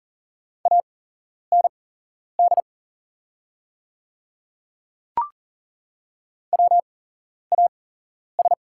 Morse code, heard as a keyed tone, about 40 wpm, spaced as for 11 wpm, 700 Hz. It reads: AND WAS
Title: 8x Character Spacing - Top 200 Words - 40wpm